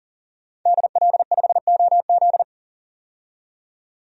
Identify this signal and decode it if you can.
Morse code, heard as a keyed tone, about 40 wpm, 700 Hz. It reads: DL5OZ
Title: DXCC-100 Call Signs - 5 Characters - 40wpm